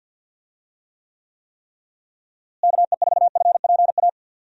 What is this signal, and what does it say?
Morse code, heard as a keyed tone, about 52 wpm, 700 Hz. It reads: KE4FPA
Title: USA Call Signs - 6 Characters - Repeated 3x - 50wpm